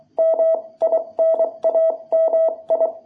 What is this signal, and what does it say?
Morse code, heard as a keyed tone, about 24 wpm, spaced as for 20 wpm, 660 Hz. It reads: MINAMI
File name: jh8cbh / 2021cw